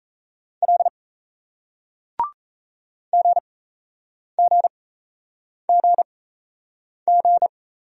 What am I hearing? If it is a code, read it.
Morse code, heard as a keyed tone, about 38 wpm, spaced as for 8 wpm, 700 Hz. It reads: L GGGG